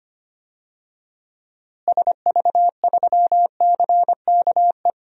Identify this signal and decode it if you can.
Morse code, heard as a keyed tone, about 25 wpm, 700 Hz. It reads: SV3CKE
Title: DXCC-100 Call Signs - 6 Characters - Repeated 3x - 25wpm